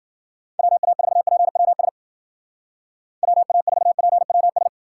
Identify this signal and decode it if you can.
Morse code, heard as a keyed tone, about 50 wpm, 700 Hz. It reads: WA4PWS WA4PWS